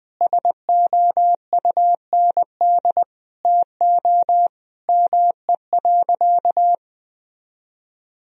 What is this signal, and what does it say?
Morse code, heard as a keyed tone, about 20 wpm, 700 Hz. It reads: SOUND TO ME.
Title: Sentences from Top 1000 Words - Review Part 1 - 20wpm